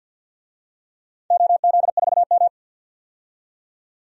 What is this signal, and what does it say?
Morse code, heard as a keyed tone, about 51 wpm, 700 Hz. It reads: OZ4M
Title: DXCC-100 Call Signs - 4 Characters - 50wpm